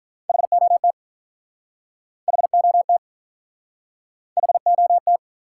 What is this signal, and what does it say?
Morse code, heard as a keyed tone, about 48 wpm, 700 Hz. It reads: HOT HOT HOT